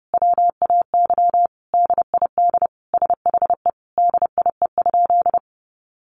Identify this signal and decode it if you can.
Morse code, heard as a keyed tone, about 30 wpm, 700 Hz. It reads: WAY DID SHE DIE?
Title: Sentences from Top 1300 Words 30wpm